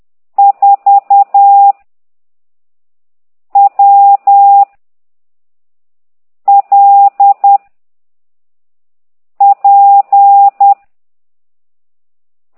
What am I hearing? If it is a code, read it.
Morse code, heard as a keyed tone, about 10 wpm, 800 Hz. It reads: 4 W L P
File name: ct2idw / exerc15